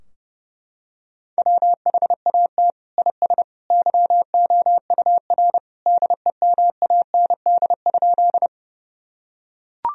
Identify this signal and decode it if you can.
Morse code, heard as a keyed tone, about 32 wpm, 700 Hz. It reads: WHAT IS YOUR DEMAND?
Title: Sentences from Top 1000 Words - Review Part 4 - 30wpm